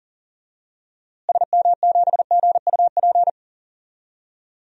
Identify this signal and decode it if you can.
Morse code, heard as a keyed tone, about 38 wpm, 700 Hz. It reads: SM7GUP
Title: DXCC-100 Call Signs - 6 Characters - Repeated 2x - 40wpm